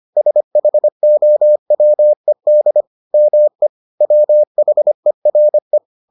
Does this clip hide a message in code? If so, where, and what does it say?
Yes: Morse code from 0.2 to 5.8 s, heard as a keyed tone, about 25 wpm, 600 Hz. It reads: SHOWED ME WHERE